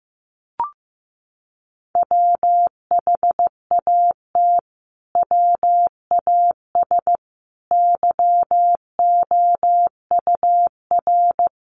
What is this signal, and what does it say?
Morse code, heard as a keyed tone, about 15 wpm, 700 Hz. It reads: WHAT WAS YOUR
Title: Sentences from Top 1500 Words 15wpm